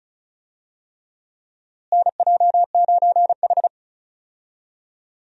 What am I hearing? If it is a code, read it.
Morse code, heard as a keyed tone, about 36 wpm, 700 Hz. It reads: NJ9H
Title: USA Call Signs - 4 Characters - Repeated 3x - 35wpm